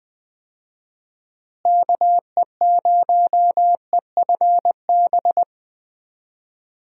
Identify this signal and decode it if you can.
Morse code, heard as a keyed tone, about 20 wpm, 700 Hz. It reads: KE0EFB
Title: USA Call Signs - 6 Characters - Repeated 3x - 20wpm